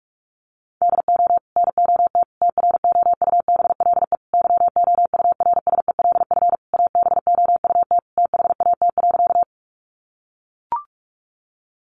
Morse code, heard as a keyed tone, about 44 wpm, 700 Hz. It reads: DO NOT TROUBLE YOURSELF ABOUT THAT.
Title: Sentences from Top 1200 Words 45wpm